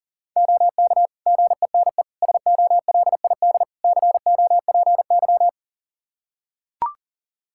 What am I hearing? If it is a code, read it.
Morse code, heard as a keyed tone, about 40 wpm, 700 Hz. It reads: OK GENE SOLID COPY
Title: Practice QSO - Exchange 3 - More Info - 40wpm